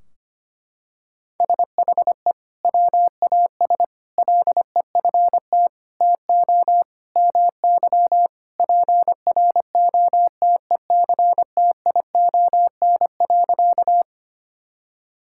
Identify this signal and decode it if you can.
Morse code, heard as a keyed tone, about 25 wpm, 700 Hz. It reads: SHE WAS LEFT TO MY PROTECTION.
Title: Sentences from Top 1600 Words 25wpm